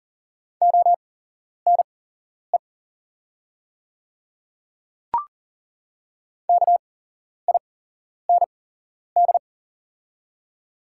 Morse code, heard as a keyed tone, about 38 wpm, spaced as for 11 wpm, 700 Hz. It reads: ONE KIND
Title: 8x Character Spacing - Top 200 Words - 40wpm